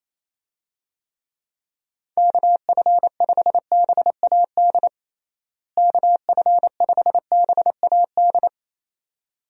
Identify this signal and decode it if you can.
Morse code, heard as a keyed tone, about 28 wpm, 700 Hz. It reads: KF5BAD KF5BAD
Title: USA Call Signs - 6 Characters - Repeated 2x - 28wpm